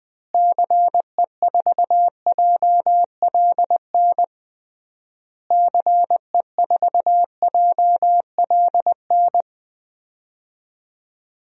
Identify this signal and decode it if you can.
Morse code, heard as a keyed tone, about 20 wpm, 700 Hz. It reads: CE4JLN CE4JLN
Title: DXCC-100 Call Signs - 6 Characters - Repeated 3x - 20wpm